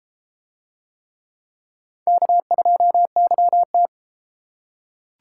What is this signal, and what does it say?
Morse code, heard as a keyed tone, about 31 wpm, 700 Hz. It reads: K2YT